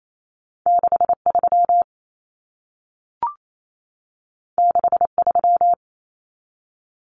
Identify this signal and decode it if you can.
Morse code, heard as a keyed tone, about 28 wpm, 700 Hz. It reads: 63 63
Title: Numbers - 2-Digit 28wpm